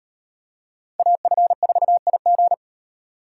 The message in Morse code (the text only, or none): AF4IG